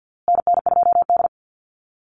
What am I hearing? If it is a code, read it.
Morse code, heard as a keyed tone, about 50 wpm, 700 Hz. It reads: NN2D
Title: USA Call Signs Repeated 2x 50wpm